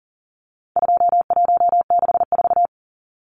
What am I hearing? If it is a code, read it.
Morse code, heard as a keyed tone, about 40 wpm, 700 Hz. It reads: 2164
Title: Numbers - 4-Digit 40wpm